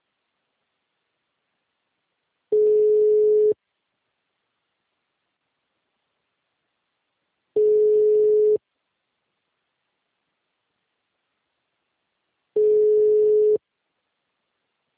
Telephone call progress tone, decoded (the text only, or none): ringback tone